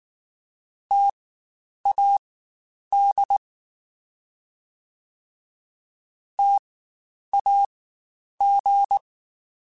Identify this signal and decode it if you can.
Morse code, heard as a keyed tone, about 19 wpm, spaced as for 9 wpm, 800 Hz. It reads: TAD TAG
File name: Spaced Alphas / Spaced Alphas mp3s / Ts sp @18